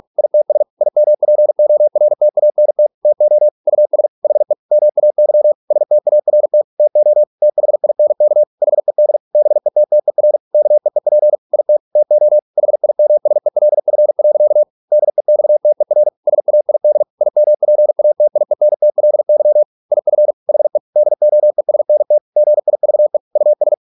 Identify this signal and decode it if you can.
Morse code, heard as a keyed tone, about 46 wpm, 600 Hz. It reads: ITS IMPORTANT TO US HE MAY START TO THINK HED BETTER KEEP IT TO HIMSELF. DEXTER SAID IMPATIENTLY IF HE DOESNT GIVE US